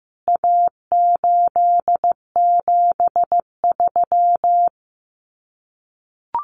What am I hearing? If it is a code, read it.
Morse code, heard as a keyed tone, about 15 wpm, 700 Hz. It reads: A873
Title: CWT Exchange 15wpm